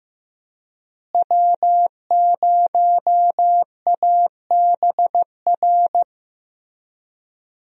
Morse code, heard as a keyed tone, about 15 wpm, 700 Hz. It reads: W0ABR
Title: USA Call Signs - 5 Characters - Repeated 3x - 15wpm